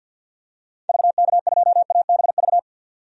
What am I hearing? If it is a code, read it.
Morse code, heard as a keyed tone, about 47 wpm, 700 Hz. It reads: VK2ABV
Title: DXCC-100 Call Signs - 6 Characters - 50wpm